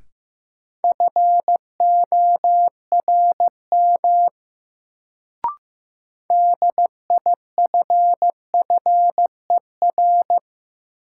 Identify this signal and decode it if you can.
Morse code, heard as a keyed tone, about 16 wpm, 700 Hz. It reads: FORM DIFFER
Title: Sentences from Top 1000 Words - Review Part 1 - New Word Drill - 15wpm